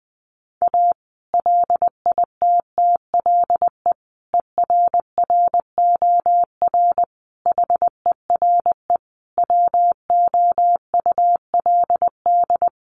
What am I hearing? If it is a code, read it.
Morse code, heard as a keyed tone, about 20 wpm, 700 Hz. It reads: A LITTLE ERROR HERE WOULD
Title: A Sound Of Thunder 20wpm